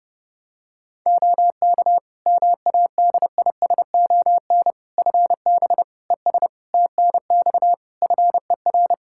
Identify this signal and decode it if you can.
Morse code, heard as a keyed tone, about 30 wpm, 700 Hz. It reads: OK MADISON FB ES TNX FER
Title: Practice QSO - Exchange 2 - Info - 30wpm